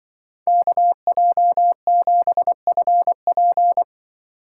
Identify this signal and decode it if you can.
Morse code, heard as a keyed tone, about 23 wpm, 700 Hz. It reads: KJ7FP
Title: Speed Racing - USA 2x2 Call Signs 20wpm